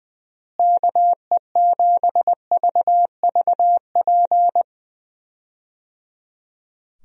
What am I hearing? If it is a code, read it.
Morse code, heard as a keyed tone, about 20 wpm, 700 Hz. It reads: KE7VVP
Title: USA Call Signs - 6 Characters - Repeated 3x - 20wpm